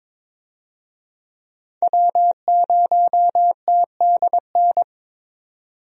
Morse code, heard as a keyed tone, about 22 wpm, 700 Hz. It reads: W0TDN
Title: USA Call Signs - 5 Characters - 22wpm